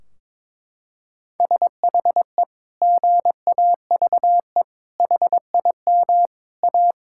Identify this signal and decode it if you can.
Morse code, heard as a keyed tone, about 22 wpm, 700 Hz. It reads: SHE GAVE HIM A
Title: Sentences from Top 1600 Words 22wpm